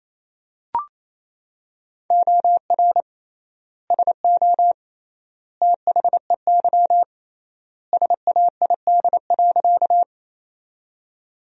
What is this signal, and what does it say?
Morse code, heard as a keyed tone, about 28 wpm, 700 Hz. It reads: OR SO THEY SAID.
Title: 3x Word-Spacing - Sentences from Top 100 Words - 28wpm